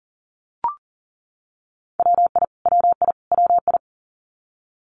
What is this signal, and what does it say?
Morse code, heard as a keyed tone, about 40 wpm, 700 Hz. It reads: WI WI WI
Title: US States Repeated 3x 40wpm